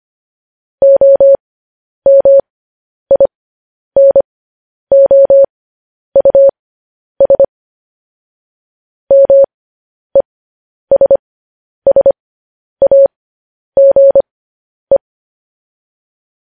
Morse code, heard as a keyed tone, about 25 wpm, spaced as for 10 wpm, 550 Hz. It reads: OMINOUS MESSAGE
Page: news-headlines-in-morse-code-at-10-wpm